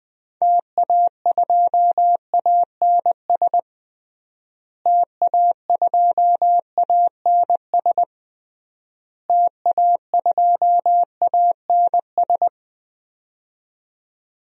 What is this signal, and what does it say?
Morse code, heard as a keyed tone, about 20 wpm, 700 Hz. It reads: TA2ANS TA2ANS TA2ANS
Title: DXCC-100 Call Signs - 6 Characters - Repeated 3x - 20wpm